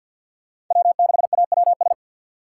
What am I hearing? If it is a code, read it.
Morse code, heard as a keyed tone, about 53 wpm, 690 Hz. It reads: W6AWS
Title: K1USN SST QSO Practice - 50wpm